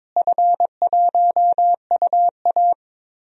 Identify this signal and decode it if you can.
Morse code, heard as a keyed tone, about 22 wpm, 700 Hz. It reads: F1UA
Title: DXCC-100 Call Signs - 4 Characters - Repeated 2x - 22wpm